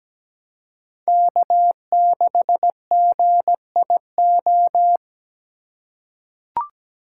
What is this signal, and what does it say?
Morse code, heard as a keyed tone, about 17 wpm, 700 Hz. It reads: K6GIO
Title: USA Call Signs - 6x Character Spacing - 17wpm